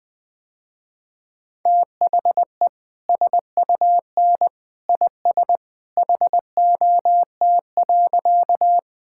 Morse code, heard as a keyed tone, about 21 wpm, 700 Hz. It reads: THE SUN IS HOT.
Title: Easy Sentences with Hints - 20wpm